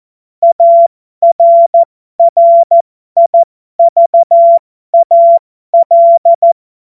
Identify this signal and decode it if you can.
Morse code, heard as a keyed tone, about 14 wpm, spaced as for 12 wpm, 680 Hz. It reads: ARRIVAL
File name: Morsekurs / Witze